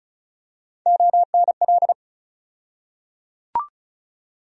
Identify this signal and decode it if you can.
Morse code, heard as a keyed tone, about 34 wpm, 700 Hz. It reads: ONL
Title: USA Call Sign Postfixes - 3 Letters 35wpm